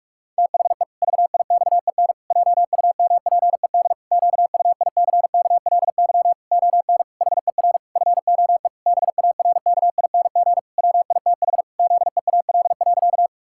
Morse code, heard as a keyed tone, about 46 wpm, 700 Hz. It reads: THE VIXEN JUMPED QUICKLY ON HER FOE BARKING WITH ZEAL.
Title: Pangram Challenge v2 - 45wpm